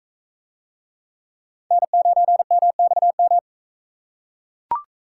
Morse code, heard as a keyed tone, about 41 wpm, 700 Hz. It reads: N9MXM